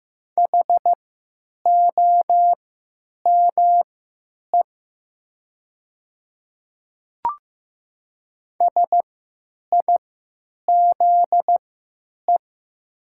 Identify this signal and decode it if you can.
Morse code, heard as a keyed tone, about 15 wpm, spaced as for 8 wpm, 700 Hz. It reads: HOME SIZE
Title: 3x Character Spacing - Top 500 Words - 15wpm